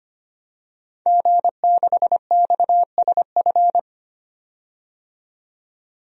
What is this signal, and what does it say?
Morse code, heard as a keyed tone, about 25 wpm, 700 Hz. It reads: G6XSF
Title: DXCC-100 Call Signs - 5 Characters - 25wpm